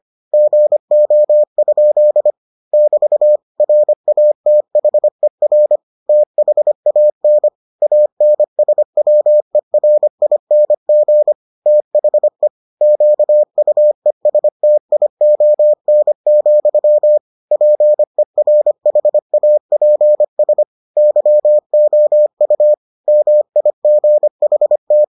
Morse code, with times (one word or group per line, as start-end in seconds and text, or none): 0.3-2.3 GO?
2.7-3.4 <BT>
3.6-5.8 RATHER
6.1-7.5 THAN
7.8-11.3 ANSWERING
11.7-12.5 THE
12.8-17.2 QUESTION,
17.5-20.6 PERHAPS
21.0-22.7 YOU
23.1-25.0 MIGHT